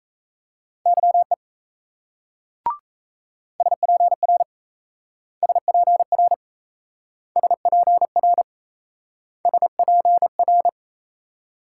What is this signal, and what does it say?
Morse code, heard as a keyed tone, about 38 wpm, 700 Hz. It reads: YE SPR SPR SPR SPR